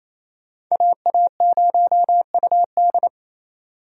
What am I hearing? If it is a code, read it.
Morse code, heard as a keyed tone, about 27 wpm, 700 Hz. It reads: AA0UD
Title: USA Call Signs - 5 Characters - Repeated 3x - 28wpm